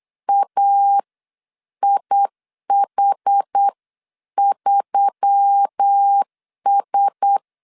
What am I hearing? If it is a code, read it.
Morse code, heard as a keyed tone, about 8 wpm, spaced as for 7 wpm, 800 Hz. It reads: AIH3S
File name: 8wpm